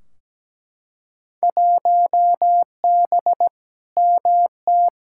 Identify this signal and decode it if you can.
Morse code, heard as a keyed tone, about 17 wpm, 700 Hz. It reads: 1B MT
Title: Field Day Exchanges - 17wpm